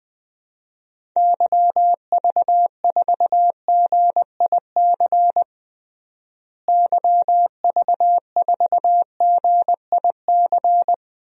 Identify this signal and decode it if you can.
Morse code, heard as a keyed tone, about 20 wpm, 700 Hz. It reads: YV4GIC YV4GIC
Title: DXCC-100 Call Signs - 6 Characters - Repeated 3x - 20wpm